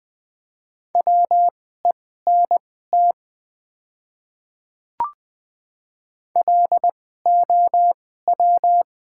Morse code, heard as a keyed tone, about 20 wpm, spaced as for 14 wpm, 700 Hz. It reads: WENT LOW
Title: 2x Character Spacing - Top 200 Words - 20wpm